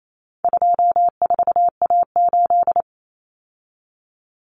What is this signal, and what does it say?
Morse code, heard as a keyed tone, about 28 wpm, 700 Hz. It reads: 24A8
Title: Common Cut Numbers - 4-Digit 28wpm